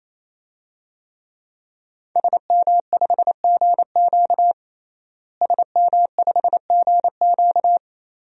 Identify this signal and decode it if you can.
Morse code, heard as a keyed tone, about 28 wpm, 700 Hz. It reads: SM5GQ SM5GQ
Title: DXCC-100 Call Signs - 5 Characters - Repeated 3x - 28wpm